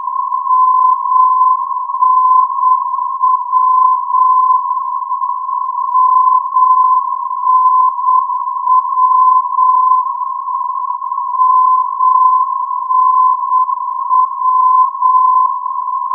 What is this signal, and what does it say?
Morse code, heard as a keyed tone, about 8 wpm, 1020 Hz. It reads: MNW MNW MNW
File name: europe / d-8524 - ndb muenchen mnw-338 (2005) (rainer brannolte, d) xc